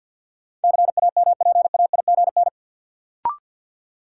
Morse code, heard as a keyed tone, about 53 wpm, 700 Hz. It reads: CAMPAIGN
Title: Sentences from Top 1200 Words - New Word Drill - 50wpm